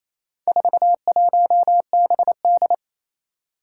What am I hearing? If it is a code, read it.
Morse code, heard as a keyed tone, about 27 wpm, 700 Hz. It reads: 41BD
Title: Four Challenging Letter-Number Combinations 28wpm